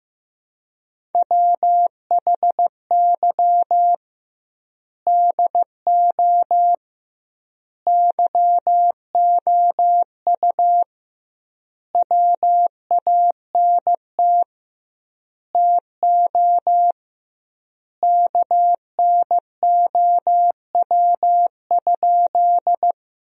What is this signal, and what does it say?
Morse code, heard as a keyed tone, about 15 wpm, 700 Hz. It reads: WHY DO YOU WANT TO KNOW?
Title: 2x Word-Spacing - Sentences from Top 200 Words - 15wpm